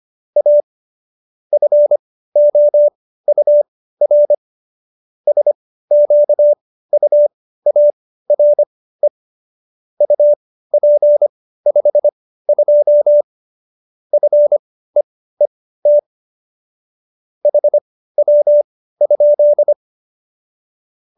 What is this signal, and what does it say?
Morse code, heard as a keyed tone, about 25 wpm, spaced as for 15 wpm, 590 Hz. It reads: A FOUR SQUARE UP 52 FEET HW?
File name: qso222 15